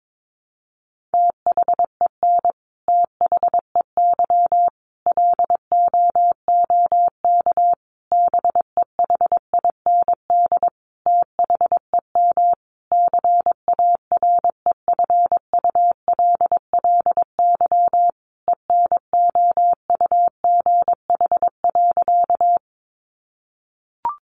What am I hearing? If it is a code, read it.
Morse code, heard as a keyed tone, about 22 wpm, 700 Hz. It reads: THEN THEY LOOK BEHIND THEM CAREFULLY ENOUGH.
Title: Sentences from Top 1300 Words 22wpm